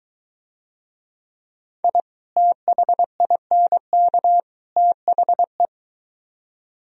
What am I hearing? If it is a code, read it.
Morse code, heard as a keyed tone, about 23 wpm, 700 Hz. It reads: I THINK THE